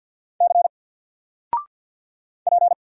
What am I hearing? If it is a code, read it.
Morse code, heard as a keyed tone, about 50 wpm, 700 Hz. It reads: C P
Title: Single Letter-Number 50wpm